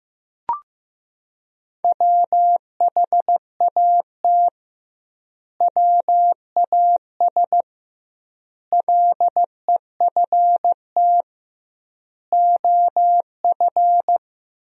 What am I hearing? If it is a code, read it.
Morse code, heard as a keyed tone, about 15 wpm, 700 Hz. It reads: WHAT WAS LEFT OF
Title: 2x Word-Spacing - Sentences from Top 300 Words - 15wpm